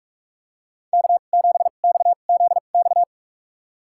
Morse code, heard as a keyed tone, about 46 wpm, spaced as for 33 wpm, 700 Hz. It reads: K7XZX
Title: USA Call Signs - 5 Characters - 2x Character Spacing - 45wpm